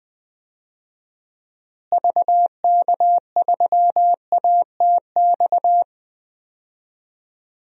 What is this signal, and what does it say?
Morse code, heard as a keyed tone, about 20 wpm, 700 Hz. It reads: VK3ATX
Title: DXCC-100 Call Signs - 6 Characters - Repeated 3x - 20wpm